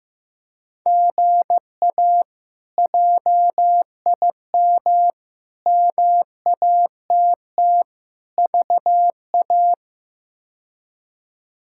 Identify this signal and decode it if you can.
Morse code, heard as a keyed tone, about 15 wpm, 700 Hz. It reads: GA JIM MATT VA
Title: K1USN SST QSO Practice - 15wpm